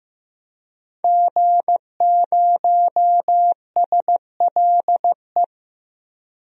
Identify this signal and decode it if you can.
Morse code, heard as a keyed tone, about 15 wpm, 700 Hz. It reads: G0SLE
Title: DXCC-100 Call Signs - 5 Characters - Repeated 2x - 15wpm